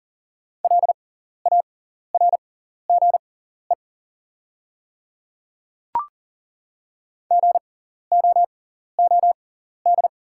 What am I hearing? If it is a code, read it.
Morse code, heard as a keyed tone, about 40 wpm, spaced as for 14 wpm, 700 Hz. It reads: LARGE GOOD
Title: 6x Character Spacing - Top 200 Words - 40wpm